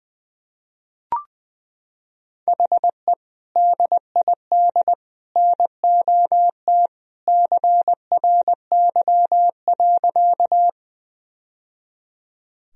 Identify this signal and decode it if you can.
Morse code, heard as a keyed tone, about 21 wpm, 700 Hz. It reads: HE DID NOT CRY.
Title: Sentences from Top 1000 Words - Review Part 2 - 20wpm